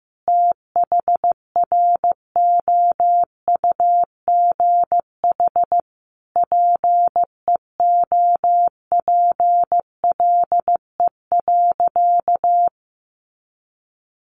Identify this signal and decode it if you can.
Morse code, heard as a keyed tone, about 15 wpm, 700 Hz. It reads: THROUGH PEOPLE.
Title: Sentences from Top 200 Words 15wpm